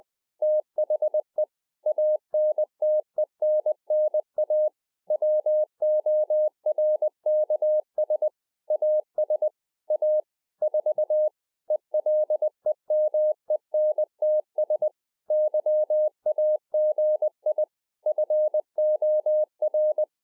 Morse code, heard as a keyed tone, about 20 wpm, 610 Hz. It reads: THE ANTENNA WORKS AS A 4 ELEMENTS YAGI FOR